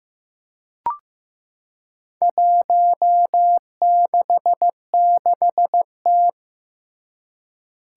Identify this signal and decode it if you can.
Morse code, heard as a keyed tone, about 15 wpm, 700 Hz. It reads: 166T
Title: Four Challenging Letter-Number Combinations 15wpm